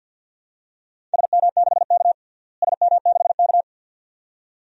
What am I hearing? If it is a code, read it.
Morse code, heard as a keyed tone, about 52 wpm, 700 Hz. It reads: SM6K SM6K
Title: DXCC-100 Call Signs - 4 Characters - Repeated 2x - 50wpm